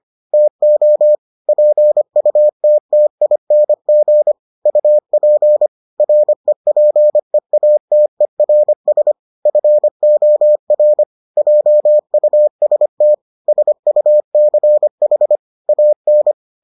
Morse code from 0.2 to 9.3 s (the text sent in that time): TO PUTTING UP REPEATERS